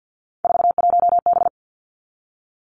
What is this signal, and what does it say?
Morse code, heard as a keyed tone, about 50 wpm, 700 Hz. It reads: 41B
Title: Three Challenging Letter-Number Combinations 50wpm